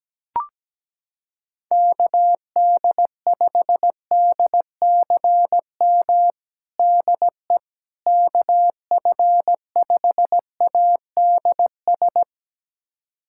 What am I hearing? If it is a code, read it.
Morse code, heard as a keyed tone, about 17 wpm, 700 Hz. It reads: KD5DCM DE KF5ADS
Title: USA Callsign DE Callsign 17wpm